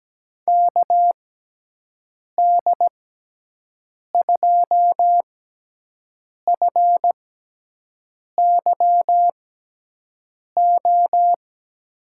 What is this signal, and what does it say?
Morse code, heard as a keyed tone, about 17 wpm, spaced as for 6 wpm, 700 Hz. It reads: KD2FYO
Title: USA Call Signs - 6 Characters - 6x Character Spacing - 17wpm